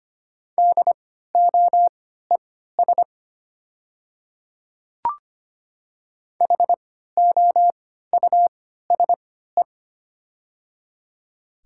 Morse code, heard as a keyed tone, about 25 wpm, spaced as for 14 wpm, 690 Hz. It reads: DOES HOUSE
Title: 3x Character Spacing - Top 500 Words - 25wpm